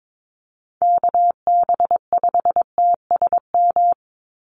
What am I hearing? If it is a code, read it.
Morse code, heard as a keyed tone, about 22 wpm, 700 Hz. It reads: KB5TSM
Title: USA Call Signs 22wpm